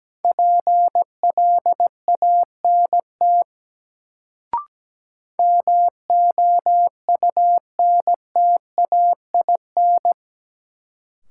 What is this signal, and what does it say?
Morse code, heard as a keyed tone, about 17 wpm, 700 Hz. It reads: PLANT MOUNTAIN
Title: Sentences from Top 1000 Words - Review Part 1 - New Word Drill - 17wpm